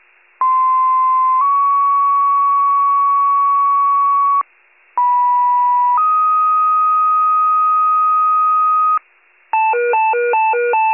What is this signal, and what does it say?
Two-tone page, heard as a one-second tone, then a three-second tone, about 1040 then 1120 Hz, about 980 then 1250 Hz. It is heard from 0.4 to 9.0 s.
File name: wcalert